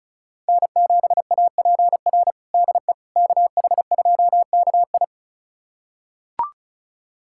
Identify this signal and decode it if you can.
Morse code, heard as a keyed tone, about 34 wpm, 700 Hz. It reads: N7APR DE KH2KI
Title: USA Callsign DE Callsign - 5 Characters - 35wpm